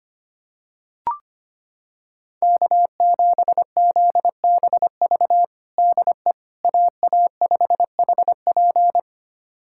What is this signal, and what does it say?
Morse code, heard as a keyed tone, about 25 wpm, 700 Hz. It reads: K7ZBV DE AA5HP
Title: USA Callsign DE Callsign - 5 Characters - 25wpm